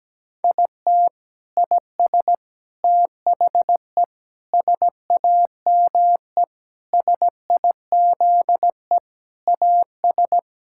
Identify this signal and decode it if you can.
Morse code, heard as a keyed tone, about 17 wpm, 700 Hz. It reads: IT IS THE SAME SIZE AS
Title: Sentences from Top 1000 Words - Review Part 6 - 17wpm